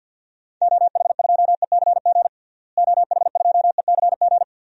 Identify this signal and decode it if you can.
Morse code, heard as a keyed tone, about 52 wpm, 700 Hz. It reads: OH2ECG OH2ECG
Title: DXCC-100 Call Signs - 6 Characters - Repeated 3x - 50wpm